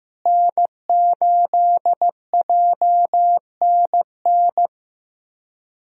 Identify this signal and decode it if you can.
Morse code, heard as a keyed tone, about 15 wpm, 700 Hz. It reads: N8JNN
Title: USA Call Signs - 5 Characters - Repeated 3x - 15wpm